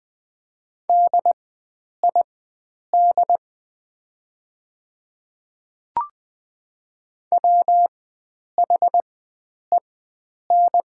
Morse code, heard as a keyed tone, about 20 wpm, spaced as for 9 wpm, 690 Hz. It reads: DID WHEN
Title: 4x Character Spacing - Top 100 Words - 20wpm